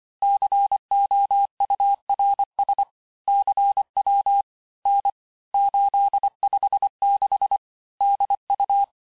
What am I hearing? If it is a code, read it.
Morse code, heard as a keyed tone, about 25 wpm, 800 Hz. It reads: COURS CW N 856 DU
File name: texte-chiffre-vitesse-23mpm